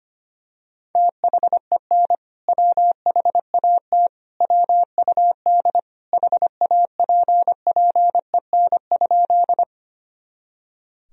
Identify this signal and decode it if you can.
Morse code, heard as a keyed tone, about 25 wpm, 700 Hz. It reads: THEN WHAT WUD HAPPEN?
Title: Sentences from Top 400 Words and Abbreviations 25wpm